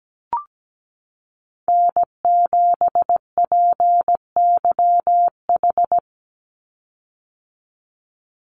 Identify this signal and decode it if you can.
Morse code, heard as a keyed tone, about 17 wpm, 700 Hz. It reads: N7PYH
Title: USA Call Signs 17wpm